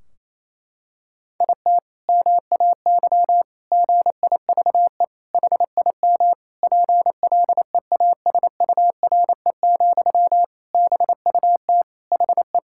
Morse code, heard as a keyed tone, about 28 wpm, 700 Hz. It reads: IT MAY GIVE HIM PLEASURE, BUT HE
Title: Sentences from Top 1600 Words 28wpm